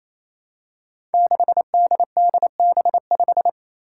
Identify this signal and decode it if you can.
Morse code, heard as a keyed tone, about 28 wpm, 700 Hz. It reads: 6DDB5
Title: Five Challenging Letter-Number Combinations 28wpm